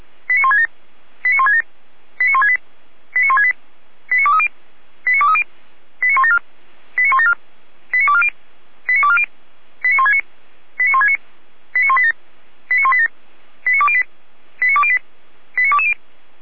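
Five-tone selective call (ZVEI), heard as quick runs of five tones, tones of about 70 ms. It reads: 89167 89167 89168 89168 89230 89230 89174 89174 89240 89240 89169 89169 89187 89187 89208 89208 89220
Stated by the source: ZVEI five-tone call received on an SDR, from the Feuerwehr Orlamünde (fire brigade) recording server.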